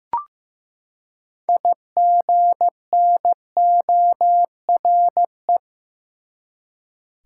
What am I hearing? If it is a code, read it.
Morse code, heard as a keyed tone, about 15 wpm, 700 Hz. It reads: IGNORE
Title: Six Letter Words - 15wpm